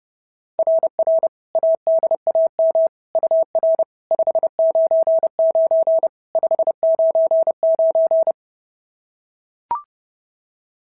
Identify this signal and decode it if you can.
Morse code, heard as a keyed tone, about 30 wpm, 690 Hz. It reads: RR ADAM UR 599 599
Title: SOTA QSO Practice 30wpm